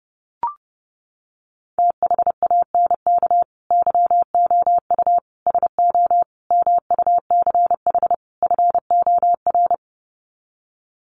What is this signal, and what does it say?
Morse code, heard as a keyed tone, about 30 wpm, 700 Hz. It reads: THANK YOU SO MUCH FOR
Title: Sets of 5 Words 30wpm